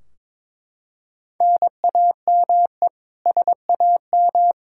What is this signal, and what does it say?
Morse code, heard as a keyed tone, about 22 wpm, 700 Hz. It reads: NAME SAM
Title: Practice QSO - Exchange 1 - Beginning - 22wpm